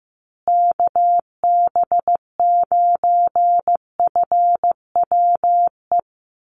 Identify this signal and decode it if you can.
Morse code, heard as a keyed tone, about 15 wpm, 700 Hz. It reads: KB9FWE
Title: USA Call Signs Repeated 2x 15wpm